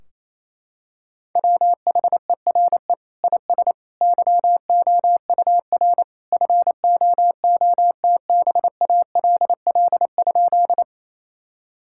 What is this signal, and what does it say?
Morse code, heard as a keyed tone, about 28 wpm, 700 Hz. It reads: WHERE IS YOUR FOOTBALL?
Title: Sentences from Top 1600 Words 28wpm